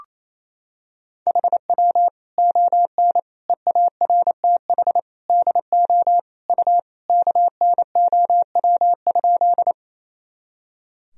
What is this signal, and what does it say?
Morse code, heard as a keyed tone, about 28 wpm, 700 Hz. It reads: HW ON EARTH DO U KNOW?
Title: Sentences from Top 200 Words and Abbreviations 28wpm